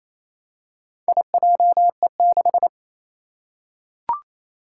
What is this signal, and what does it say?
Morse code, heard as a keyed tone, about 28 wpm, 700 Hz. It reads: IJE6
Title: Four Challenging Letter-Number Combinations 28wpm